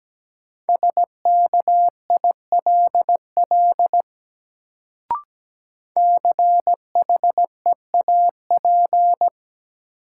Morse code, heard as a keyed tone, about 17 wpm, 700 Hz. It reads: SKILL CHEAP
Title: Sentences from Top 1200 Words - New Word Drill - 17wpm